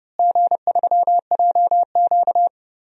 Morse code, heard as a keyed tone, about 30 wpm, 700 Hz. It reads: G3JQ
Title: DXCC-100 Call Signs - 4 Characters - Repeated 3x - 30wpm